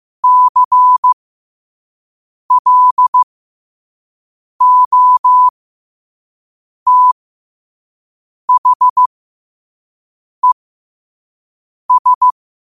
Morse code, heard as a keyed tone, about 15 wpm, spaced as for 5 wpm, 1000 Hz. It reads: CLOTHES
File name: qotdfiles / 5wpm